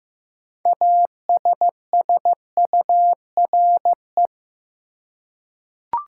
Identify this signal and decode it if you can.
Morse code, heard as a keyed tone, about 15 wpm, 700 Hz. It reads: ASSURE
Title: Six Letter Words - 15wpm